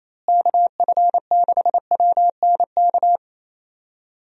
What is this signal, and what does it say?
Morse code, heard as a keyed tone, about 28 wpm, 700 Hz. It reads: KF6WNK
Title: USA Call Signs - 6 Characters - Repeated 3x - 28wpm